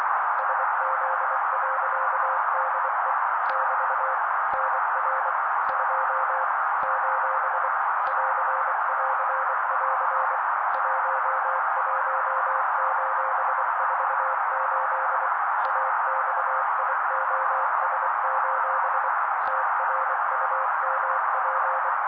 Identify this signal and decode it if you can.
Morse code, heard as a keyed tone, about 24 wpm, 530 Hz. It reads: SZ.DE <BT> NR 2 : <AR><AR><AR> 11:48 AXIOS: TRUMP
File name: daily / 23wpm noise